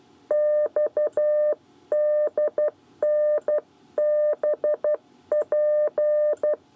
Morse code, heard as a keyed tone, about 12 wpm, 600 Hz. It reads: XDNBP
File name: 8 bis lezione cw 1